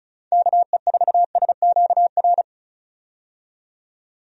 Morse code, heard as a keyed tone, about 34 wpm, 700 Hz. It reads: KE4SQR
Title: USA Call Signs - 6 Characters - 35wpm